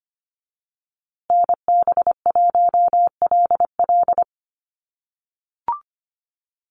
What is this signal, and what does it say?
Morse code, heard as a keyed tone, about 25 wpm, 700 Hz. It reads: NB1LL
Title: USA Call Signs Repeated 3x 25wpm